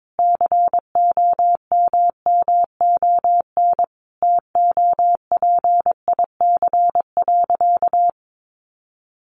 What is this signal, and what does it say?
Morse code, heard as a keyed tone, about 22 wpm, 700 Hz. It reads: COMMON TOPIC.
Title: Sentences from Top 1200 Words 22wpm